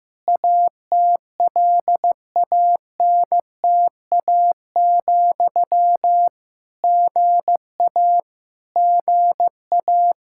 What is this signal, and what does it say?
Morse code, heard as a keyed tone, about 15 wpm, 700 Hz. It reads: ATLANTA, GA GA
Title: Practice QSO - Exchange 1 - Beginning - 15wpm